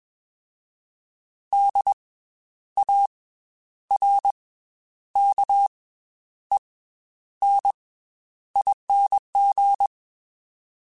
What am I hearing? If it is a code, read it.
Morse code, heard as a keyed tone, about 21 wpm, 800 Hz. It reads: D A R K E N ING
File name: Special INGs / Special INGs-mp3s / Slow ing @20